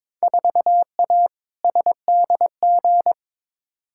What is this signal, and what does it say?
Morse code, heard as a keyed tone, about 22 wpm, 700 Hz. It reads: 4A SDG
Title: Field Day Exchanges - 22wpm